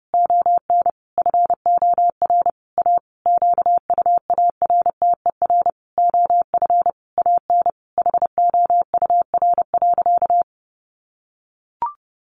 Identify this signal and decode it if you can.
Morse code, heard as a keyed tone, about 30 wpm, 700 Hz. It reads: ON FOR A QUARTER OF AN HOUR.
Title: Sentences from Top 1500 Words 30wpm